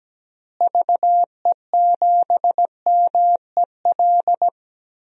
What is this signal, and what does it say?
Morse code, heard as a keyed tone, about 17 wpm, 700 Hz. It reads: VE7MEL
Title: DXCC-100 Call Signs - 6 Characters - Repeated 3x - 17wpm